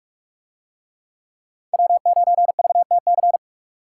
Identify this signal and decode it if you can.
Morse code, heard as a keyed tone, about 42 wpm, 700 Hz. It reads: W9VTC